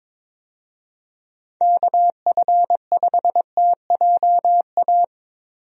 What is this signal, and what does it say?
Morse code, heard as a keyed tone, about 21 wpm, 700 Hz. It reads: KF5TJA
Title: USA Call Signs - 6 Characters - Repeated 2x - 22wpm